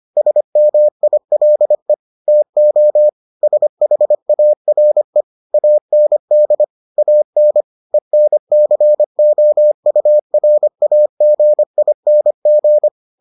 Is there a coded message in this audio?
Yes: Morse code from 0.2 to 12.9 s, heard as a keyed tone, about 25 wpm, 600 Hz. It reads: SMILE TO SHARE AND AN ENCOURAGING